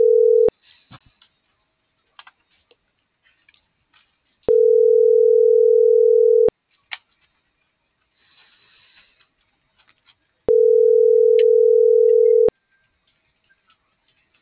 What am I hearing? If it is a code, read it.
Ringback tone (North America), 440 + 480 Hz.